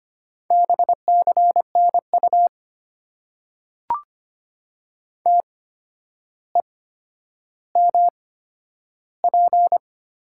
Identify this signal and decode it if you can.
Morse code, heard as a keyed tone, about 25 wpm, spaced as for 11 wpm, 700 Hz. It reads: BCNU TEMP
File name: Top 90 QSO Elements - 8x Character Spacing - 25wpm